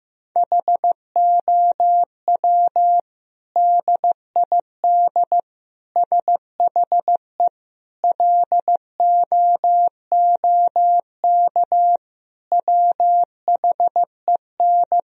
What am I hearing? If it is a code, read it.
Morse code, heard as a keyed tone, about 15 wpm, 700 Hz. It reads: HOW DID SHE LOOK WHEN
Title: Sentences from Top 1000 Words - Review Part 3 - 15wpm